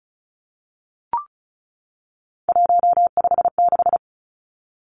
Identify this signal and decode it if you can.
Morse code, heard as a keyed tone, about 34 wpm, 700 Hz. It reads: 156
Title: Numbers - 3-Digit 35wpm